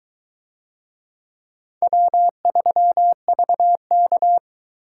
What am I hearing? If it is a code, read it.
Morse code, heard as a keyed tone, about 21 wpm, 700 Hz. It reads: W3VK